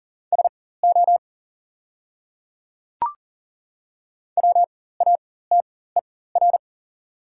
Morse code, heard as a keyed tone, about 40 wpm, spaced as for 19 wpm, 690 Hz. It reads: SO WATER
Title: 4x Character Spacing - Top 100 Words - 40wpm